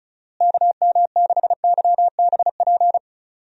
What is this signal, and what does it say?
Morse code, heard as a keyed tone, about 36 wpm, 700 Hz. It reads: KM6YBP
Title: USA Call Signs - 6 Characters - Repeated 3x - 35wpm